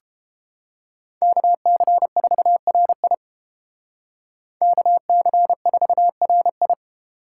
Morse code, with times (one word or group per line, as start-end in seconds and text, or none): 1.2-3.1 KC4RI
4.6-6.7 KC4RI